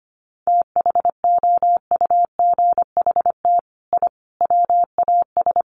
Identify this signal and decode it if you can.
Morse code, heard as a keyed tone, about 25 wpm, 700 Hz. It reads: THOUGHT I WAS
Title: Sets of 5 Words Encore2 25wpm